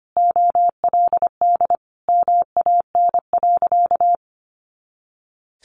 Morse code, heard as a keyed tone, about 25 wpm, 700 Hz. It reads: OLD MAN.
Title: Sentences from Top 500 Words 25wpm